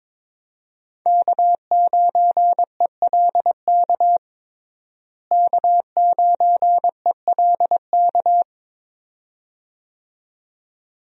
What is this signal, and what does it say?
Morse code, heard as a keyed tone, about 22 wpm, 700 Hz. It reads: K9ELK K9ELK
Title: USA Call Signs - 5 Characters - Repeated 2x - 22wpm